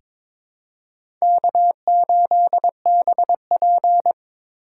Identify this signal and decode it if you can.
Morse code, heard as a keyed tone, about 22 wpm, 700 Hz. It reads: K8BP
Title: USA Call Signs - 4 Characters - 3x Character Spacing - 22wpm